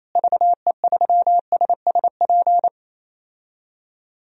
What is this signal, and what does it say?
Morse code, heard as a keyed tone, about 28 wpm, 700 Hz. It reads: VE3SSP
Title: DXCC-100 Call Signs - 6 Characters - Repeated 3x - 28wpm